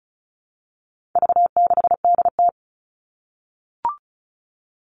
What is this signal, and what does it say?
Morse code, heard as a keyed tone, about 34 wpm, 700 Hz. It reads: V6DT
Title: All Cut Numbers - 4-Digit 35wpm